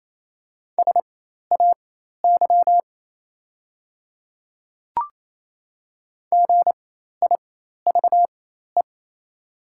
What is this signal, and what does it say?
Morse code, heard as a keyed tone, about 28 wpm, spaced as for 13 wpm, 690 Hz. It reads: SAY GIVE
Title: 4x Character Spacing - Top 200 Words - 28wpm